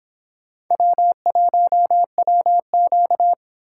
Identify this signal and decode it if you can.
Morse code, heard as a keyed tone, about 25 wpm, 700 Hz. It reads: W1WQ